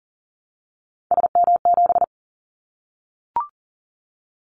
Morse code, heard as a keyed tone, about 39 wpm, 700 Hz. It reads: SM7
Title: Callsign Prefixes for Top 25 DXCC - 40wpm